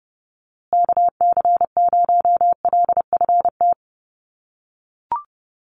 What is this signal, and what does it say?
Morse code, heard as a keyed tone, about 30 wpm, 700 Hz. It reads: KC0LFT
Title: USA Call Signs Repeated 2x 30wpm